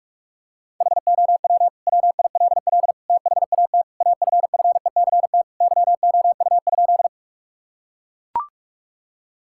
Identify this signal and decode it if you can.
Morse code, heard as a keyed tone, about 42 wpm, 700 Hz. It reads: HOW WILL THAT AFFECT YOU?
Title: Sentences from Top 1000 Words - Review Part 6 - 45wpm